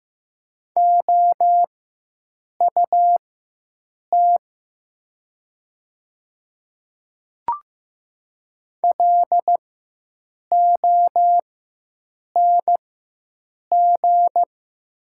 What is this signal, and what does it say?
Morse code, heard as a keyed tone, about 15 wpm, spaced as for 7 wpm, 700 Hz. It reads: OUT LONG
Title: 4x Character Spacing - Top 100 Words - 15wpm